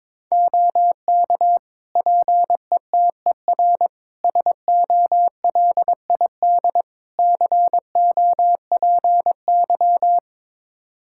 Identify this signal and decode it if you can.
Morse code, heard as a keyed tone, about 22 wpm, 700 Hz. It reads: OK PETER SOLID COPY
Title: K1USN SST QSO Practice 22wpm